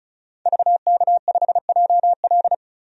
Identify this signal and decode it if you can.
Morse code, heard as a keyed tone, about 36 wpm, 700 Hz. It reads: VK5JL
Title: DXCC-100 Call Signs - 5 Characters - Repeated 3x - 35wpm